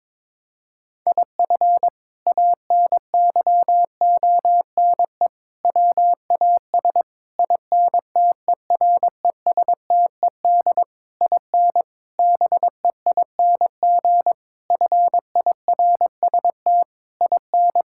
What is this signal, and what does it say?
Morse code, heard as a keyed tone, about 22 wpm, 700 Hz. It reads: IF ANYONE WAS INTERESTED IN BEING FIRST IN
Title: Sentences from Top 1600 Words 22wpm